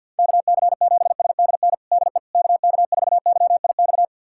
Morse code, heard as a keyed tone, about 46 wpm, 700 Hz. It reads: KC7SDN DE KK4YIX
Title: USA Callsign DE Callsign - 6 Characters - 50wpm